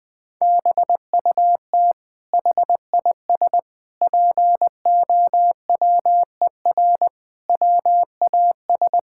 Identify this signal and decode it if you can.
Morse code, heard as a keyed tone, about 20 wpm, 700 Hz. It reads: BUT HIS POWER WAS
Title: Sentences from Top 1000 Words - Review Part 6 - 20wpm